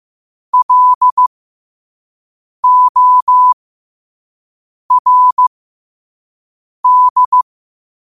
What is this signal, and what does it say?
Morse code, heard as a keyed tone, about 15 wpm, spaced as for 5 wpm, 1000 Hz. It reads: LORD
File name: qotdfiles / 5wpm